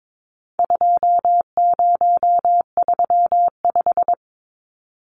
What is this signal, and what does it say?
Morse code, heard as a keyed tone, about 22 wpm, 700 Hz. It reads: 2035
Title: Numbers - 4-Digit 22wpm